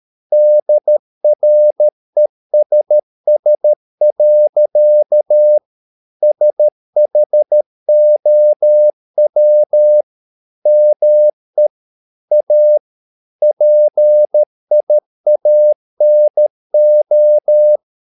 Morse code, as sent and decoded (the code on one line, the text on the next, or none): -.. .-. ... ... .-.-.- / ... .... --- .-- / -- . / .- / .--. .. .- -. ---
DRESS. SHOW ME A PIANO